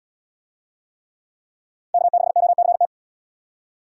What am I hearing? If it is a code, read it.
Morse code, heard as a keyed tone, about 76 wpm, 700 Hz. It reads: K6OCT